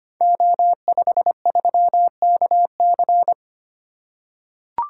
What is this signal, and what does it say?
Morse code, heard as a keyed tone, about 25 wpm, 700 Hz. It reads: O53KC
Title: Code Groups - 5 Letter:Numbers - 25wpm